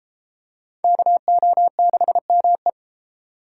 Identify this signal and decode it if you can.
Morse code, heard as a keyed tone, about 29 wpm, 700 Hz. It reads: KO6ME